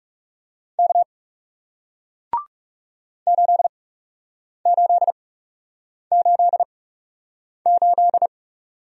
Morse code, heard as a keyed tone, about 41 wpm, spaced as for 8 wpm, 700 Hz. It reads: K 8888